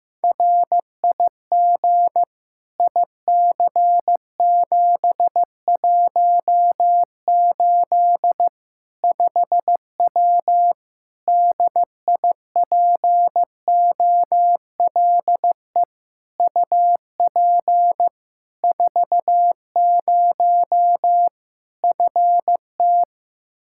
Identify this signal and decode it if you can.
Morse code, heard as a keyed tone, about 15 wpm, 700 Hz. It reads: RIG IC718 5W DIPOLE UP 40 FT
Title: K1USN SST QSO Practice 15wpm